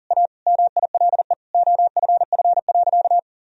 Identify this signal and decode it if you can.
Morse code, heard as a keyed tone, about 40 wpm, 700 Hz. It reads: A MILE OFF.
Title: Sentences from Top 1000 Words - Review Part 1 - 40wpm